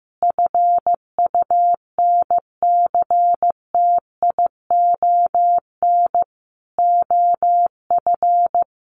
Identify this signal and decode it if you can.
Morse code, heard as a keyed tone, about 14 wpm, 700 Hz. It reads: FUNCTION OF
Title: Sentences from Top 1100 Words 15wpm